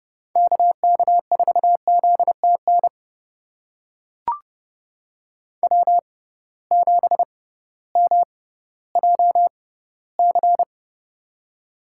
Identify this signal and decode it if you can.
Morse code, heard as a keyed tone, about 31 wpm, spaced as for 16 wpm, 700 Hz. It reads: KK4ZTN W7MJC
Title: USA Call Signs - 6x Character Spacing - 30wpm